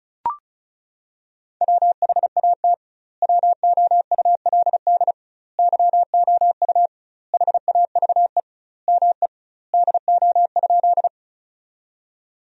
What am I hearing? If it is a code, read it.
Morse code, heard as a keyed tone, about 36 wpm, 700 Hz. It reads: WHAT WOULD YOU HAVE ME DO?
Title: 2x Word-Spacing - Sentences from Top 200 Words - 35wpm